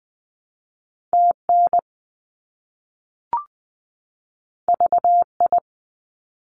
Morse code, heard as a keyed tone, about 20 wpm, 700 Hz. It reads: TN VI
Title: ARRL Sections 20wpm